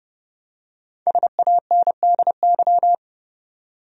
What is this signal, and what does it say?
Morse code, heard as a keyed tone, about 30 wpm, 700 Hz. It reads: SANDY
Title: USA Common Names - Max 5 Letters - 30wpm